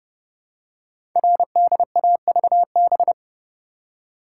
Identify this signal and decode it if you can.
Morse code, heard as a keyed tone, about 30 wpm, 700 Hz. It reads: RDAVB
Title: Code Groups - 5 Letters - 30wpm